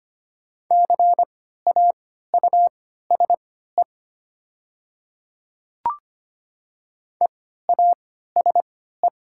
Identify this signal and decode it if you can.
Morse code, heard as a keyed tone, about 25 wpm, spaced as for 14 wpm, 700 Hz. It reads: CAUSE EASE
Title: 3x Character Spacing - Top 300 Words - 25wpm